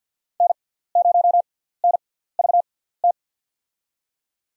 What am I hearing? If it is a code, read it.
Morse code, heard as a keyed tone, about 51 wpm, spaced as for 17 wpm, 700 Hz. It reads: N0NVT
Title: USA Call Signs - 5 Characters - 6x Character Spacing - 50wpm